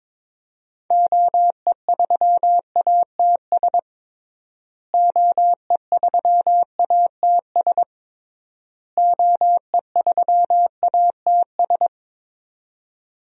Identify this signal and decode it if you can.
Morse code, heard as a keyed tone, about 22 wpm, 700 Hz. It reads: OE3ATS OE3ATS OE3ATS
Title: DXCC-100 Call Signs - 6 Characters - Repeated 3x - 22wpm